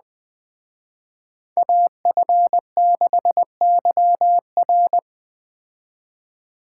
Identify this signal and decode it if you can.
Morse code, heard as a keyed tone, about 20 wpm, 700 Hz. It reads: AF6YR